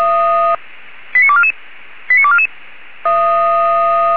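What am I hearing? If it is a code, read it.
Five-tone selective call (ZVEI), heard as quick runs of five tones, tones of about 70 ms. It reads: 89244 89244
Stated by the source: ZVEI five-tone call received via SDR.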